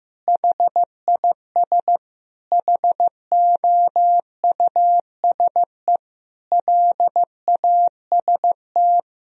Morse code, heard as a keyed tone, about 15 wpm, 700 Hz. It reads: HIS HOUSE LAST
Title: Sentences from Top 1000 Words - Review Part 1 - 15wpm